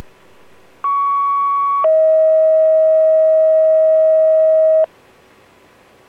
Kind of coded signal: two-tone page